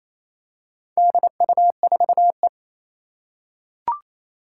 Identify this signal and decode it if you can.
Morse code, heard as a keyed tone, about 29 wpm, 700 Hz. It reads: DU4E
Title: Four Challenging Letter-Number Combinations 28wpm